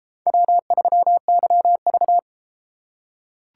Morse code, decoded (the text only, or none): W3YV